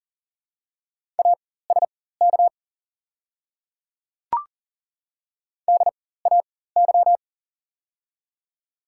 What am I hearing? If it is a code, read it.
Morse code, heard as a keyed tone, about 40 wpm, spaced as for 19 wpm, 690 Hz. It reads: ASK DAY
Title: 4x Character Spacing - Top 200 Words - 40wpm